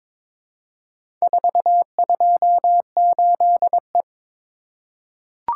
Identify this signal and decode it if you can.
Morse code, heard as a keyed tone, about 22 wpm, 700 Hz. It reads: 428E
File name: practice / 22wpm / Four Letter-Number Combinations - 22wpm